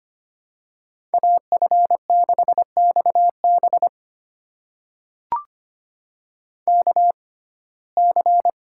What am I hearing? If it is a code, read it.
Morse code, heard as a keyed tone, about 25 wpm, spaced as for 18 wpm, 700 Hz. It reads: AF6XB KC
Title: USA Call Signs - 5 Characters - 6x Character Spacing - 25wpm